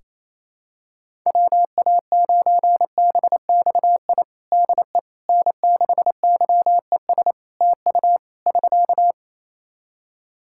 Morse code, heard as a keyed tone, about 27 wpm, 700 Hz. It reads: WA9BXI DE N6YES TU <SK>
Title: Practice QSO - Exchange 4 - Ending - 28wpm